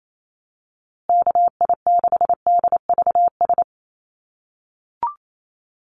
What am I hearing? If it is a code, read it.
Morse code, heard as a keyed tone, about 28 wpm, 700 Hz. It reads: KI6DVS
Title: USA Call Signs Repeated 2x 28wpm